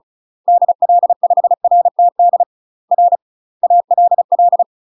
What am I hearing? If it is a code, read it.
Morse code, heard as a keyed tone, about 36 wpm, 700 Hz. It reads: DL5RTD R ALL
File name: UK MASTER QSOs TRAINING/35WPM